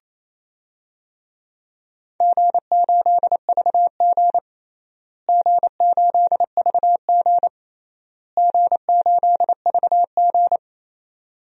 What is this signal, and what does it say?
Morse code, heard as a keyed tone, about 27 wpm, 700 Hz. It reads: G8VG G8VG G8VG
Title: DXCC-100 Call Signs - 4 Characters - Repeated 3x - 28wpm